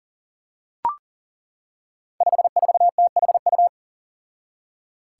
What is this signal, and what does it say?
Morse code, heard as a keyed tone, about 37 wpm, 700 Hz. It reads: 54THU
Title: Five Challenging Letter-Number Combinations 40wpm